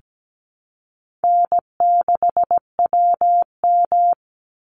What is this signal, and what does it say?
Morse code, heard as a keyed tone, about 17 wpm, 700 Hz. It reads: N6WM
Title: Call Signs Repeated 3x - 17wpm